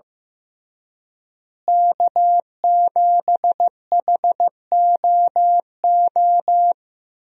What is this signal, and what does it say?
Morse code, heard as a keyed tone, about 16 wpm, 700 Hz. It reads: K7HOO
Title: USA Call Signs - 5 Characters - Repeated 3x - 15wpm